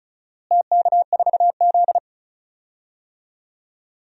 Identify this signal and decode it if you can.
Morse code, heard as a keyed tone, about 34 wpm, 700 Hz. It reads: TK4Z
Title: DXCC-100 Call Signs - 4 Characters - Repeated 2x - 35wpm